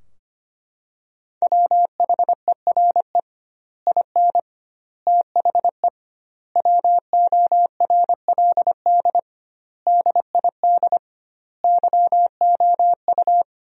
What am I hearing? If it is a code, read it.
Morse code, heard as a keyed tone, about 25 wpm, 700 Hz. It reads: WHERE IN THE WORLD DID YOU
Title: 2x Word-Spacing - Sentences from Top 200 Words - 25wpm